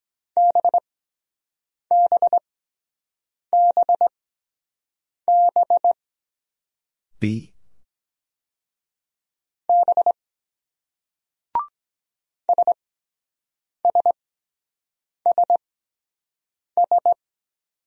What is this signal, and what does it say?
Morse code, heard as a keyed tone, about 23 wpm, spaced as for 7 wpm, 700 Hz. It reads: BBBB B SSSS